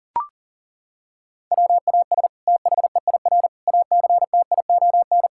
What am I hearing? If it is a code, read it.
Morse code, heard as a keyed tone, about 40 wpm, 700 Hz. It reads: WAS THEIR ACTION
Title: Sentences from Top 1000 Words - Review Part 6 - 40wpm